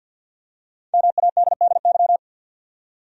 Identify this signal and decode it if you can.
Morse code, heard as a keyed tone, about 52 wpm, 700 Hz. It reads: MADDY
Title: USA Common Names - Max 5 Letters - 50wpm